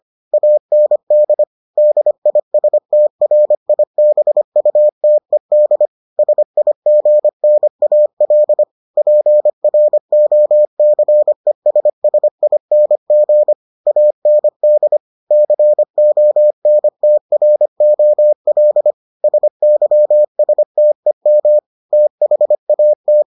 Morse code, heard as a keyed tone, about 25 wpm, 600 Hz. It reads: AND DISTRIBUTED SIGNAL PROCESSING AND CONTROL SYSTEM THAT